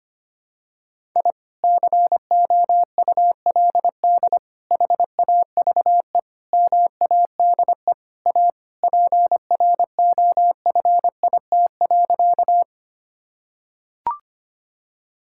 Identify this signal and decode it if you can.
Morse code, heard as a keyed tone, about 25 wpm, 700 Hz. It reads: I COULD HAVE MADE A PROFIT.
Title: Sentences from Top 1000 Words - Review Part 5 - 25wpm